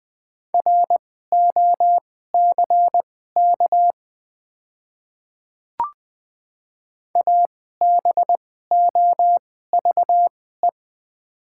Morse code, heard as a keyed tone, about 20 wpm, spaced as for 14 wpm, 690 Hz. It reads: ROCK ABOVE
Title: 2x Character Spacing - Top 500 Words - 20wpm